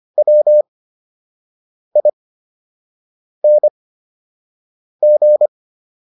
Morse code, heard as a keyed tone, about 25 wpm, spaced as for 6 wpm, 600 Hz. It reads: WING